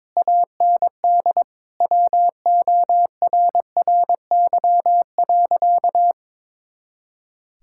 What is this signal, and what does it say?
Morse code, heard as a keyed tone, about 22 wpm, 700 Hz. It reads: AND WORRY.
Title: Pangram Challenge v2 - 22wpm